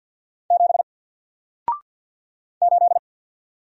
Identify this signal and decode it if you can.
Morse code, heard as a keyed tone, about 50 wpm, 700 Hz. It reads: 7 8
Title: Single Letter-Number 50wpm